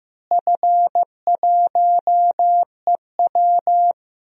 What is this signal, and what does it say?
Morse code, heard as a keyed tone, about 15 wpm, 700 Hz. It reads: F1EW
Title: DXCC-100 Call Signs - 4 Characters - Repeated 3x - 15wpm